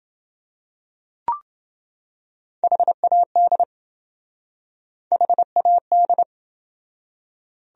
Morse code, heard as a keyed tone, about 28 wpm, 700 Hz. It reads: HAD HAD